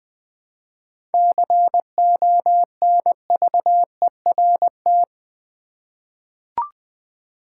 Morse code, heard as a keyed tone, about 20 wpm, 700 Hz. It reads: CONVERT
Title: Seven Letter Words - 20wpm